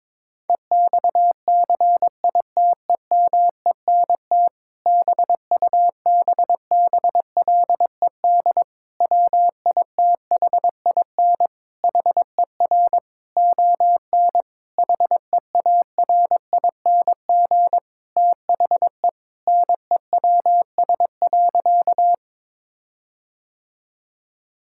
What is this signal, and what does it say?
Morse code, heard as a keyed tone, about 22 wpm, 700 Hz. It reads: EXCITEMENT BUBBLED WITHIN HER ON HEARING THE NEWS.
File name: Difficult Sentences with Hints - 22wpm